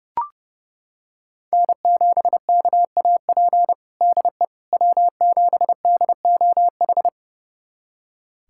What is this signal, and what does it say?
Morse code, heard as a keyed tone, about 32 wpm, 700 Hz. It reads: N7KAP DE W7DOH
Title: USA Callsign DE Callsign - 5 Characters - 30wpm